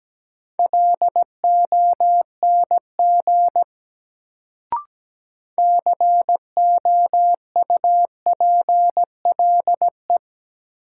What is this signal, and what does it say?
Morse code, heard as a keyed tone, about 17 wpm, 700 Hz. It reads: LONG COUPLE
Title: Sets of 2 Words - New Word Drill - 17wpm